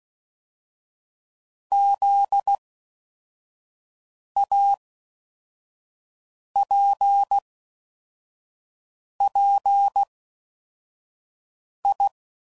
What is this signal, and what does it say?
Morse code, heard as a keyed tone, about 16 wpm, 800 Hz. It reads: Z A P P I